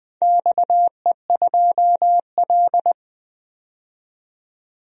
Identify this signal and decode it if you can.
Morse code, heard as a keyed tone, about 20 wpm, 700 Hz. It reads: XE2L
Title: DXCC-100 Call Signs - 4 Characters - 20wpm